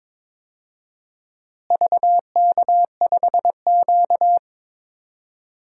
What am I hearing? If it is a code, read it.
Morse code, heard as a keyed tone, about 22 wpm, 700 Hz. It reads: VK5Q
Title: DXCC-100 Call Signs - 4 Characters - Repeated 2x - 22wpm